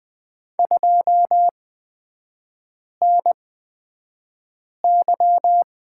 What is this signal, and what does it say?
Morse code, heard as a keyed tone, about 20 wpm, spaced as for 5 wpm, 700 Hz. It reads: 2NY
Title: 229 Lesson 37e - Letter x - Three Characters 20wpm